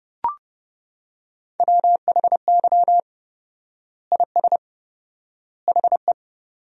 Morse code, heard as a keyed tone, about 30 wpm, 700 Hz. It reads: WHY IS HE
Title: 4x Word-Spacing - Sentences from Top 200 Words - 30wpm